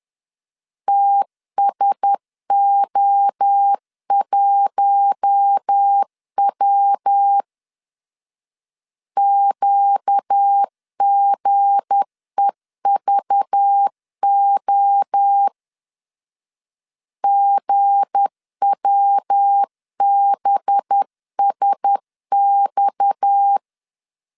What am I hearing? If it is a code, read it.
Morse code, heard as a keyed tone, about 10 wpm, 800 Hz. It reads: TSO1W QGEVO GWBSX
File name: LEZIONI / 7 GF9Z / 7 G 10wpm 2501